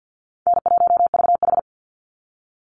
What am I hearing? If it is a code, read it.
Morse code, heard as a keyed tone, about 50 wpm, 700 Hz. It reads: N1VH
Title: Easy Call Signs Repeated 2x - 50wpm